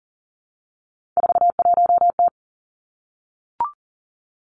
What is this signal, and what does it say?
Morse code, heard as a keyed tone, about 40 wpm, 700 Hz. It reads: 41T
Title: Three Challenging Letter-Number Combinations 40wpm